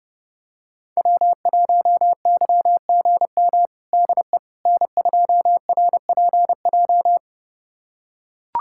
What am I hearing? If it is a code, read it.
Morse code, heard as a keyed tone, about 29 wpm, 700 Hz. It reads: W1YGM DE N2RPJ
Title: USA Callsign DE Callsign - 5 Characters - 30wpm